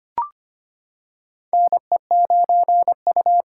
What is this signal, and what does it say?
Morse code, heard as a keyed tone, about 25 wpm, 700 Hz. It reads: NE9U
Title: K1USN SST QSO Practice - 25wpm